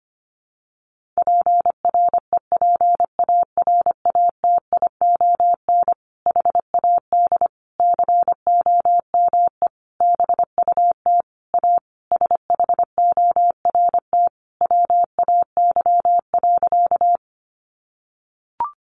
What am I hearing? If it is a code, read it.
Morse code, heard as a keyed tone, about 25 wpm, 700 Hz. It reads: PREPARATION HAD COME BUT A SHORT WAY.
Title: Sentences from Top 1400 Words 25wpm